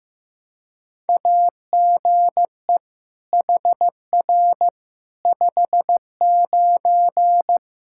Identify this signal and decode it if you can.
Morse code, heard as a keyed tone, about 15 wpm, 700 Hz. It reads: AGE HR 59
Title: Practice QSO - Exchange 3 - More Info - 15wpm